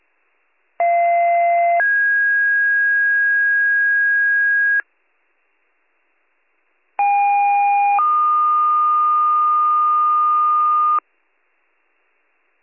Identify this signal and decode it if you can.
Two-tone page, heard as a one-second tone, then a three-second tone, about 690 then 1650 Hz, about 810 then 1220 Hz. It is heard from 0.8 to 11.0 s.